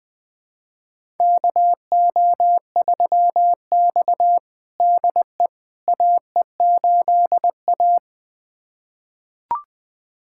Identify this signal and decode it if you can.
Morse code, heard as a keyed tone, about 20 wpm, 700 Hz. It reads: KO3X DE AE8A
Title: USA Callsign DE Callsign - 4 Characters - 20wpm